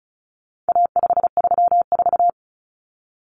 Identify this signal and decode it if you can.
Morse code, heard as a keyed tone, about 34 wpm, 700 Hz. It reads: A534
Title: Common Cut Numbers - 4-Digit 35wpm